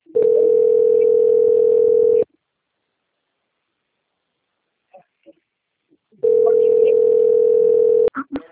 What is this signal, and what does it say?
Ringback tone (North America), 440 + 480 Hz.
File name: out- - - - - .19514